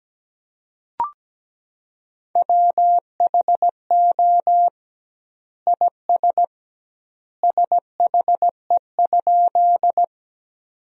Morse code, heard as a keyed tone, about 17 wpm, 700 Hz. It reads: WHO IS SHE?
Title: 2x Word-Spacing - Sentences from Top 100 Words - 17wpm